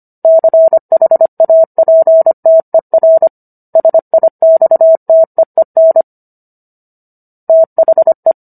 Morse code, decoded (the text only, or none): CHAPTER SIXTEEN THE